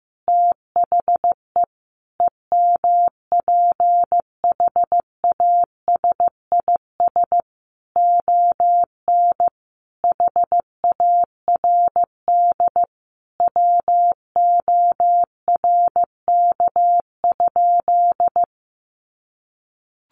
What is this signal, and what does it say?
Morse code, heard as a keyed tone, about 15 wpm, 700 Hz. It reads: THE EMPHASIS ON HARD WORK?
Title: Sentences from Top 1400 Words 15wpm